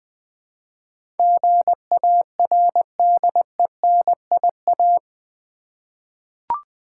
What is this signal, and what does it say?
Morse code, heard as a keyed tone, about 20 wpm, 700 Hz. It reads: GARDENIA
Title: Difficult Words with Hints - 20wpm